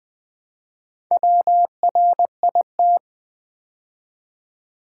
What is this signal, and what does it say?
Morse code, heard as a keyed tone, about 20 wpm, 700 Hz. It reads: WRIT
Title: Build A Word - 2 Letters At A Time - 6-Letter Words - 20wpm